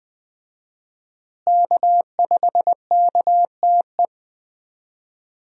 Morse code, heard as a keyed tone, about 20 wpm, 700 Hz. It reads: K5KTE
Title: USA Call Signs - 5 Characters - Repeated 3x - 20wpm